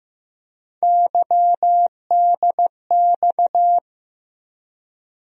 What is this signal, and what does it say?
Morse code, heard as a keyed tone, about 15 wpm, 700 Hz. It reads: YDX
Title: USA Call Sign Postfixes - 3 Letters 15wpm